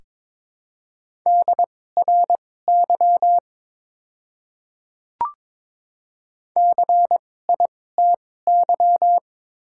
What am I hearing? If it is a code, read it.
Morse code, heard as a keyed tone, about 22 wpm, spaced as for 16 wpm, 700 Hz. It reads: DRY CITY
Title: 2x Character Spacing - Top 500 Words - 22wpm